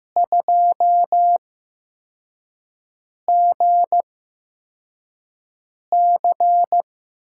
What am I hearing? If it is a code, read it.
Morse code, heard as a keyed tone, about 15 wpm, spaced as for 4 wpm, 700 Hz. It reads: 2GC
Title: USA Call Signs - 8x Character Spacing - 15wpm